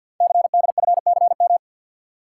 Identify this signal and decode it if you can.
Morse code, heard as a keyed tone, about 48 wpm, 700 Hz. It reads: CDFCM
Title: Code Groups - 5 Letters - 50wpm